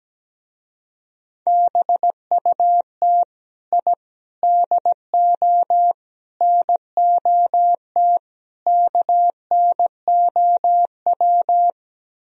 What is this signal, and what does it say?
Morse code, heard as a keyed tone, about 17 wpm, 700 Hz. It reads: BUT I DO NOT KNOW
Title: Speed-Racing - Sets of 5 Words 17wpm